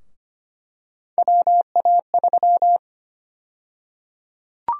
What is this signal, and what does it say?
Morse code, heard as a keyed tone, about 25 wpm, 700 Hz. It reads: WA3
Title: USA Call Sign Prefixes - 25wpm